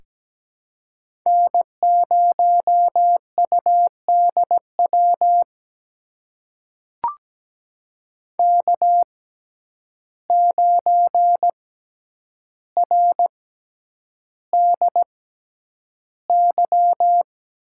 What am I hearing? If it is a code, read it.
Morse code, heard as a keyed tone, about 17 wpm, spaced as for 9 wpm, 700 Hz. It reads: N0UDW K9RDY
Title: USA Call Signs - 5 Characters - 6x Character Spacing - 17wpm